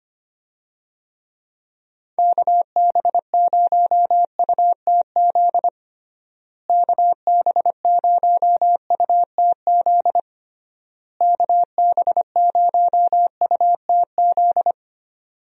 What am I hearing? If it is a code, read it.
Morse code, heard as a keyed tone, about 25 wpm, 700 Hz. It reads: KB0UTZ KB0UTZ KB0UTZ
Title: USA Call Signs - 6 Characters - Repeated 3x - 25wpm